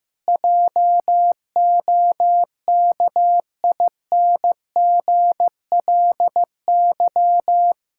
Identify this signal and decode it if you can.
Morse code, heard as a keyed tone, about 15 wpm, 700 Hz. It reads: JOKINGLY
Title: Pangram Challenge v2 - 15wpm